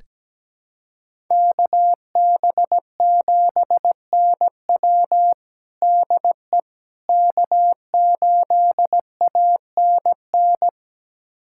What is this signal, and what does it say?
Morse code, heard as a keyed tone, about 17 wpm, 700 Hz. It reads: KB7NW DE K8ANN
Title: USA Callsign DE Callsign - 5 Characters - 17wpm